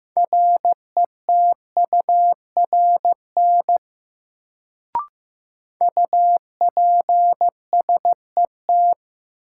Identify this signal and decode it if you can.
Morse code, heard as a keyed tone, about 15 wpm, 700 Hz. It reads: RETURN UPSET
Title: Sentences from Top 800 Words - New Word Drill - 15wpm